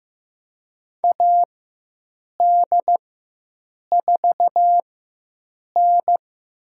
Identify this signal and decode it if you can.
Morse code, heard as a keyed tone, about 15 wpm, spaced as for 7 wpm, 700 Hz. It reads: AD4N
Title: USA Call Signs - 4x Character Spacing - 15wpm